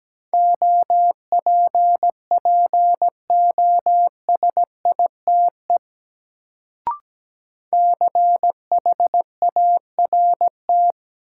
Morse code, heard as a keyed tone, about 17 wpm, 700 Hz. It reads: OPPOSITE CHART
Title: Sentences from Top 1300 Words - New Word Drill - 17wpm